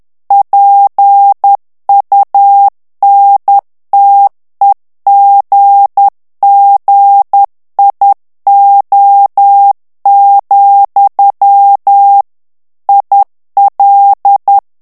Morse code, heard as a keyed tone, about 10 wpm, 800 Hz. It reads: PUNTEGGIO, IL
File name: ITALIEN texte clair 2 vitesse 10 mots minute